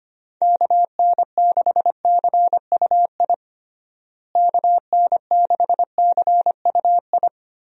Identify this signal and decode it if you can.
Morse code, heard as a keyed tone, about 25 wpm, 700 Hz. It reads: KN6CUI KN6CUI
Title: USA Call Signs - 6 Characters - Repeated 3x - 25wpm